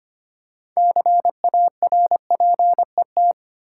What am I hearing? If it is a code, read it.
Morse code, heard as a keyed tone, about 25 wpm, 700 Hz. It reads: CARPET
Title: Six Letter Words - 25wpm